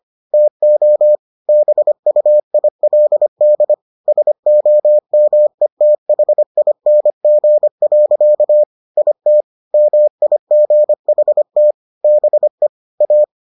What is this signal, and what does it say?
Morse code, heard as a keyed tone, about 25 wpm, 600 Hz. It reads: TO BUILD SOMETHING. IT MIGHT BE A